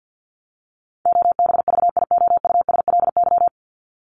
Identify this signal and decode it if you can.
Morse code, heard as a keyed tone, about 50 wpm, 700 Hz. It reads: OBVIOUSLY
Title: Top 1501-1750 Words 50wpm